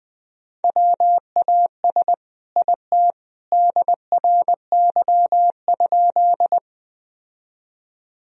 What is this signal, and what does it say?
Morse code, heard as a keyed tone, about 20 wpm, 700 Hz. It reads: WAS IT DRY?
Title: Sentences from Top 1000 Words - Review Part 3 - 20wpm